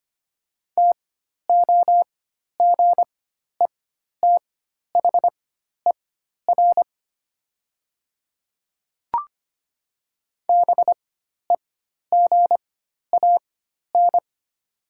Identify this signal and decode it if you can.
Morse code, heard as a keyed tone, about 25 wpm, spaced as for 12 wpm, 700 Hz. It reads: TOGETHER BEGAN
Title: 4x Character Spacing - Top 300 Words - 25wpm